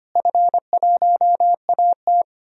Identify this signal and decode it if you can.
Morse code, heard as a keyed tone, about 25 wpm, 700 Hz. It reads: F1AT
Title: DXCC-100 Call Signs - 4 Characters - Repeated 3x - 25wpm